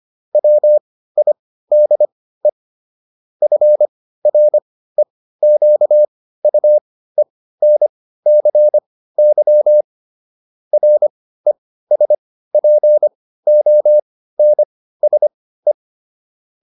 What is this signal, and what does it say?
Morse code, heard as a keyed tone, about 25 wpm, spaced as for 15 wpm, 600 Hz. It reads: WIDE FREQUENCY RESPONSE
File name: SS203 15